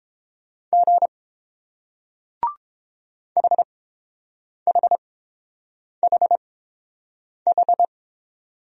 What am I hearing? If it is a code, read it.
Morse code, heard as a keyed tone, about 30 wpm, spaced as for 7 wpm, 700 Hz. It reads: G HHHH